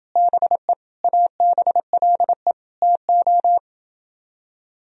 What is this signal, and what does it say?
Morse code, heard as a keyed tone, about 31 wpm, 700 Hz. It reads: BE ABLE TO